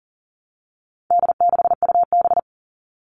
Morse code, heard as a keyed tone, about 40 wpm, 700 Hz. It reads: D6UB
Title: All Cut Numbers - 4-Digit 40wpm